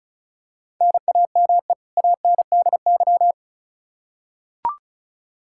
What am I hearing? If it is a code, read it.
Morse code, heard as a keyed tone, about 36 wpm, 700 Hz. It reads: NAME ANDY
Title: Practice QSO - Exchange 1 - Beginning - 35wpm